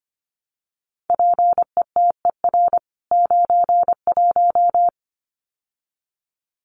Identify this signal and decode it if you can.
Morse code, heard as a keyed tone, about 25 wpm, 700 Hz. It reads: PETER 91
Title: CWT Exchange 25wpm